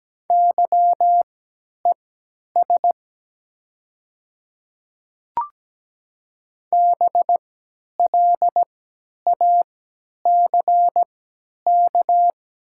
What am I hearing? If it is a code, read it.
Morse code, heard as a keyed tone, about 17 wpm, spaced as for 10 wpm, 690 Hz. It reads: YES BLACK
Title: 3x Character Spacing - Top 500 Words - 17wpm